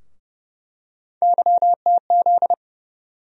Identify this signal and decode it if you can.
Morse code, heard as a keyed tone, about 30 wpm, 700 Hz. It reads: YTZ
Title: USA Call Sign Postfixes - 3 Letters 30wpm